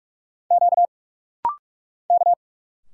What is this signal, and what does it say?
Morse code, heard as a keyed tone, about 46 wpm, 700 Hz. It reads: Q K
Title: Single Letters - Rapid-Fire 45wpm